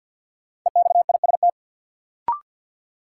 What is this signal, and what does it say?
Morse code, heard as a keyed tone, about 51 wpm, 700 Hz. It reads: EXIST
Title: Sentences from Top 1000 Words - Review Part 5 - New Word Drill - 50wpm